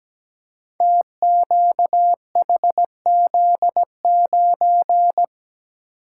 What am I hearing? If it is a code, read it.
Morse code, heard as a keyed tone, about 17 wpm, 700 Hz. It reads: TQHZ9
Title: Code Groups - 5 Letter:Numbers - 17wpm